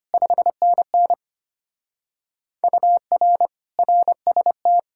Morse code, heard as a keyed tone, about 31 wpm, 700 Hz. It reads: 5NN UR RST
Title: Speed-Racing Common QSO Phrases - 25wpm